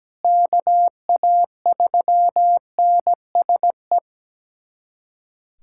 Morse code, heard as a keyed tone, about 17 wpm, 700 Hz. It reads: KA3NSE
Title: USA Callsign DE Callsign - 6 Characters - 17wpm